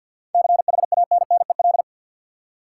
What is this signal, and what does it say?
Morse code, heard as a keyed tone, about 53 wpm, 700 Hz. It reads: CHANNEL
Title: Seven Letter Words - 50wpm